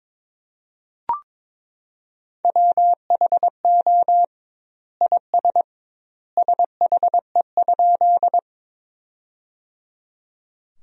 Morse code, heard as a keyed tone, about 22 wpm, 700 Hz. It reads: WHO IS SHE?
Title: 2x Word-Spacing - Sentences from Top 100 Words - 22wpm